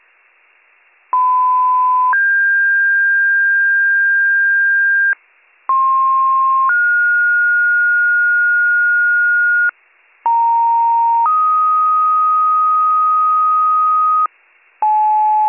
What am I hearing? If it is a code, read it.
Two-tone page, heard as a one-second tone, then a three-second tone, about 1010 then 1600 Hz, about 1060 then 1430 Hz, about 930 then 1250 Hz, with noise. It is heard from 1.1 to 14.3 s.